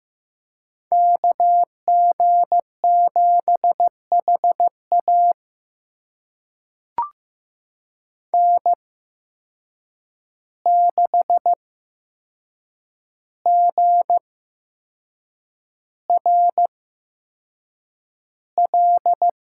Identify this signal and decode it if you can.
Morse code, heard as a keyed tone, about 15 wpm, spaced as for 6 wpm, 700 Hz. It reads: KG7HA N6GRL
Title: USA Call Signs - 5 Characters - 8x Character Spacing - 15wpm